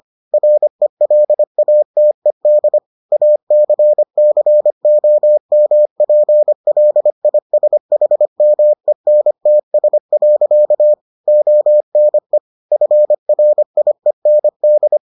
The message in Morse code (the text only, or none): RELATED ACCOMPLISHMENTS. ONE FRIEND